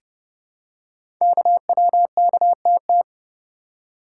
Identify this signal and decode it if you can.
Morse code, heard as a keyed tone, about 30 wpm, 700 Hz. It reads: KWKTT
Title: Code Groups - 5 Letters - 30wpm